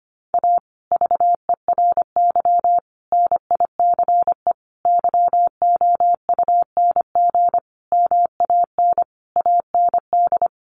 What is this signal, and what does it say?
Morse code, heard as a keyed tone, about 25 wpm, 700 Hz. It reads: A VERY NICE YOUNG MAN AND
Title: Sentences from Top 1200 Words 25wpm